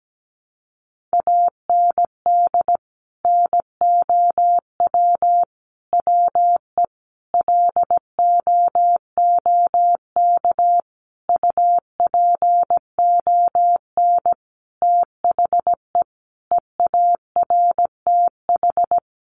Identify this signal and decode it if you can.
Morse code, heard as a keyed tone, about 17 wpm, 700 Hz. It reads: AND NOW WE LOOK UPON THE EARTH
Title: Sentences from Top 1200 Words 17wpm